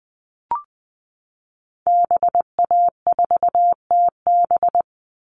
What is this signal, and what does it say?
Morse code, heard as a keyed tone, about 20 wpm, 700 Hz. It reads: BA4TB
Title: Call Signs Repeated 2x - 20wpm